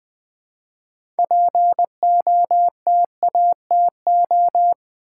Morse code, heard as a keyed tone, about 20 wpm, 700 Hz. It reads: POTATO
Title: Six Letter Words - 20wpm